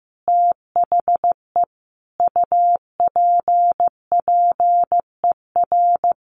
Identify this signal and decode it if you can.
Morse code, heard as a keyed tone, about 14 wpm, 700 Hz. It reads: THE UPPER
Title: A Sound Of Thunder 15wpm